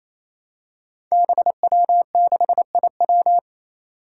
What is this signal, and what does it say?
Morse code, heard as a keyed tone, about 28 wpm, 700 Hz. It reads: BW6IW
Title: Five Challenging Letter-Number Combinations 28wpm